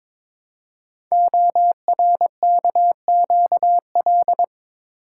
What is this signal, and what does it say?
Morse code, heard as a keyed tone, about 22 wpm, 700 Hz. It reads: ORKQL
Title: Code Groups - 5 Letters - 22wpm